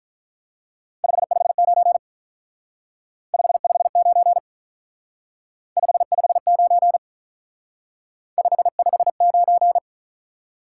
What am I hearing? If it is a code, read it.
Morse code, heard as a keyed tone, about 44 wpm, 700 Hz. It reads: 559 559 559 559